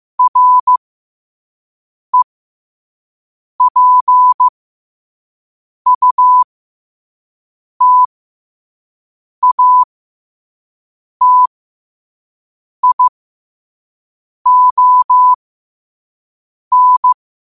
Morse code, heard as a keyed tone, about 15 wpm, spaced as for 5 wpm, 1000 Hz. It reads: REPUTATION
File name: qotdfiles / 5wpm